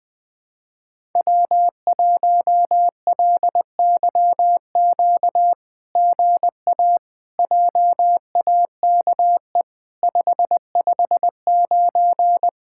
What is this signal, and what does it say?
Morse code, heard as a keyed tone, about 20 wpm, 700 Hz. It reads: W1LYQ GA JAKE 559
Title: SOTA QSO Practice 20wpm